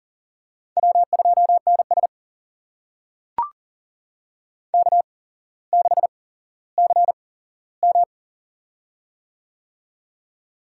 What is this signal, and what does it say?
Morse code, heard as a keyed tone, about 42 wpm, spaced as for 17 wpm, 700 Hz. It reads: W2NS K6CM
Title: USA Call Signs - 4 Characters - 8x Character Spacing - 40wpm